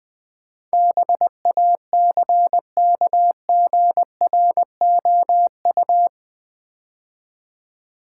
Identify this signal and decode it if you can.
Morse code, heard as a keyed tone, about 21 wpm, 700 Hz. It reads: BACKGROU
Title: Build A Word - 4 Letters At A Time - 10-Letter Words - 20wpm